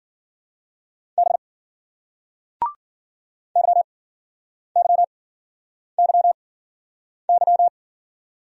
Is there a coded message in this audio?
Yes: Morse code from 1.2 to 7.7 s, heard as a keyed tone, about 50 wpm, spaced as for 9 wpm, 700 Hz. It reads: B YYYY